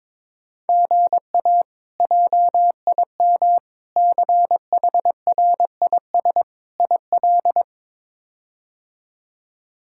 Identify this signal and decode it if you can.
Morse code, heard as a keyed tone, about 22 wpm, 700 Hz. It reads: GA JIM CHRIS IL
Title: K1USN SST QSO Practice - 22wpm